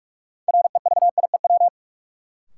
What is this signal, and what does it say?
Morse code, heard as a keyed tone, about 46 wpm, 700 Hz. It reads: REVIEW
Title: Sentences from Top 900 Words - New Word Drill - 45wpm